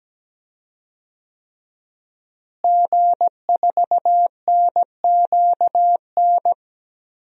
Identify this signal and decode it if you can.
Morse code, heard as a keyed tone, about 17 wpm, 700 Hz. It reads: G4NQN
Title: DXCC-100 Call Signs - 5 Characters - Repeated 3x - 17wpm